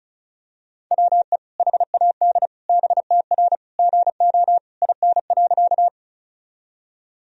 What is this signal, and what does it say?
Morse code, heard as a keyed tone, about 36 wpm, 700 Hz. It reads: WE HAD BTR GO IN.
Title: Sentences from Top 400 Words and Abbreviations 35wpm